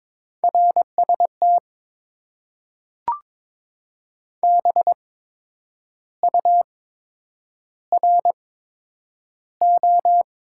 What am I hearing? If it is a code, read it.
Morse code, heard as a keyed tone, about 22 wpm, spaced as for 8 wpm, 700 Hz. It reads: RST BURO
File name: Top 90 QSO Elements - 8x Character Spacing - 22wpm